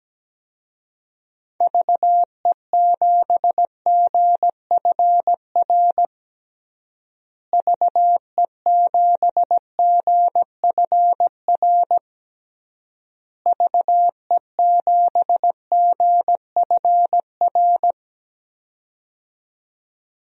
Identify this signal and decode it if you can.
Morse code, heard as a keyed tone, about 17 wpm, 700 Hz. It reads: VE7GFR VE7GFR VE7GFR
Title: DXCC-100 Call Signs - 6 Characters - Repeated 3x - 17wpm